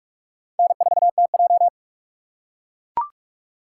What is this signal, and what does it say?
Morse code, heard as a keyed tone, about 44 wpm, 700 Hz. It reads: N4TJ